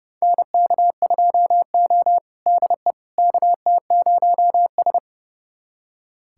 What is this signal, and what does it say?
Morse code, heard as a keyed tone, about 30 wpm, 700 Hz. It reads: NK2O DE KT0S
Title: USA Callsign DE Callsign - 4 Characters - 30wpm